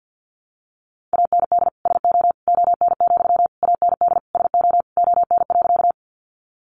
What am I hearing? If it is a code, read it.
Morse code, heard as a keyed tone, about 50 wpm, 700 Hz. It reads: AND SO ON, AND SO ON.
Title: Sentences from Top 100 Words 50wpm